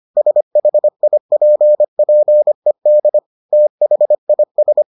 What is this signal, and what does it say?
Morse code, heard as a keyed tone, about 25 wpm, 600 Hz. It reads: SHIPPED THIS